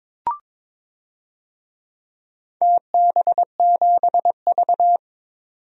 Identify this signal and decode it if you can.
Morse code, heard as a keyed tone, about 22 wpm, 700 Hz. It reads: TB7V
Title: DXCC-100 Call Signs - 4 Characters - 22wpm